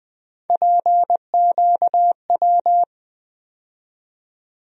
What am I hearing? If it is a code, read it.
Morse code, heard as a keyed tone, about 20 wpm, 700 Hz. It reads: PQW
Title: USA Call Sign Postfixes - 3 Letters 20wpm